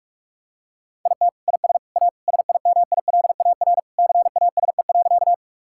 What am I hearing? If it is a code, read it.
Morse code, heard as a keyed tone, about 41 wpm, 700 Hz. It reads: IT IS A SIMILAR CASE.
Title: Sentences from Top 1000 Words - Review Part 5 - 45wpm